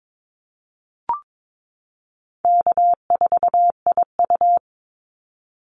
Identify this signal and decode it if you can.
Morse code, heard as a keyed tone, about 22 wpm, 700 Hz. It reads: K4IU
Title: Call Signs - 22wpm